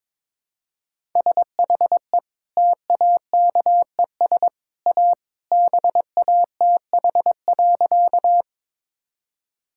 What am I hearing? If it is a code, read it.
Morse code, heard as a keyed tone, about 22 wpm, 700 Hz. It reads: SHE TAKES A BATH.
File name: practice / 22wpm / Easy Sentences with Hints - 22wpm